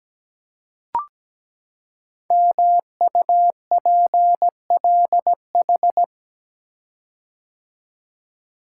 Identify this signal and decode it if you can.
Morse code, heard as a keyed tone, about 17 wpm, 700 Hz. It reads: MUPLH
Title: Code Groups - 5 Letters - 17wpm